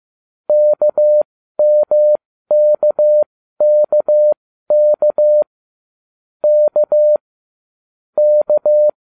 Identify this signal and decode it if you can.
Morse code, heard as a keyed tone, about 15 wpm, spaced as for 12 wpm, 600 Hz. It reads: KMKKK K K